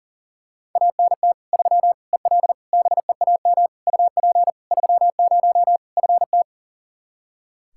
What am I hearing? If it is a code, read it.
Morse code, heard as a keyed tone, about 40 wpm, 700 Hz. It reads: ANT 3 EL BEAM UP 30 FT
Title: Practice QSO - Exchange 2 - Info - 40wpm